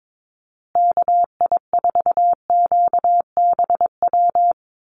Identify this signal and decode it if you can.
Morse code, heard as a keyed tone, about 22 wpm, 700 Hz. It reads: KI4QBW
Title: USA Call Signs Repeated 2x 22wpm